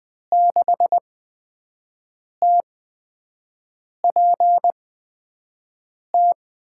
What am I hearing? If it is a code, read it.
Morse code, heard as a keyed tone, about 20 wpm, spaced as for 5 wpm, 700 Hz. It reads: 6TPT
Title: USA Call Signs - 5 Characters - 8x Character Spacing - 20wpm